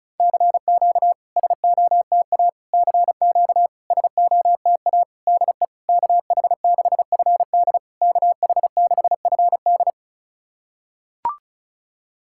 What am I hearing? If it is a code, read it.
Morse code, heard as a keyed tone, about 36 wpm, 690 Hz. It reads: CQ SOTA CQ SOTA DE KH6FD KH6FD
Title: SOTA QSO Practice 35wpm